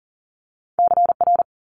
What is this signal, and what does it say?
Morse code, heard as a keyed tone, about 42 wpm, 690 Hz. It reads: CR
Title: Multi-letter Phonemes 40wpm